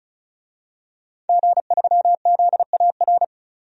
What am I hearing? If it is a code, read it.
Morse code, heard as a keyed tone, about 36 wpm, 700 Hz. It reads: G3ZAR
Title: DXCC-100 Call Signs - 5 Characters - 35wpm